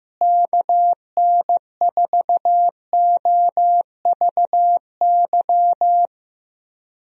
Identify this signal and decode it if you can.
Morse code, heard as a keyed tone, about 15 wpm, 700 Hz. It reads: KN4OVY
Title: USA Call Signs - 6 Characters - Repeated 3x - 15wpm